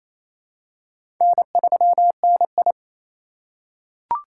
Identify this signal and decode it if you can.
Morse code, heard as a keyed tone, about 28 wpm, 700 Hz. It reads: N3NI
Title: USA Call Signs - 4x Character Spacing - 28wpm